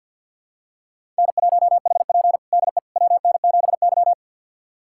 Morse code, heard as a keyed tone, about 48 wpm, 700 Hz. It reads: N1HP DE WN7Y
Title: USA Callsign DE Callsign - 4 Characters - 50wpm